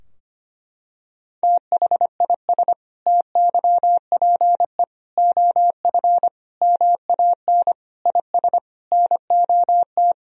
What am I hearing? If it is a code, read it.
Morse code, heard as a keyed tone, about 25 wpm, 700 Hz. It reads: THIS TYPE OF MAN IS NOT
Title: Sentences from Top 1000 Words - Review Part 3 - 25wpm